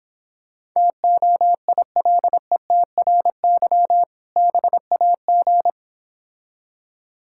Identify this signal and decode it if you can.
Morse code, heard as a keyed tone, about 26 wpm, 700 Hz. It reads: TOILETRY BAG